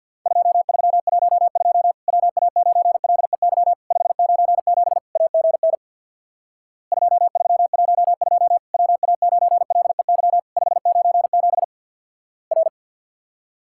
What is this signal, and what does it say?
Morse code, heard as a keyed tone, about 49 wpm, 700 Hz. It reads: 2312 WA9LEY 597 AGN 2312 WA9LEY 597 R
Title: Worked All Europe DX Contest - QTC Exchange with Repeats - 50wpm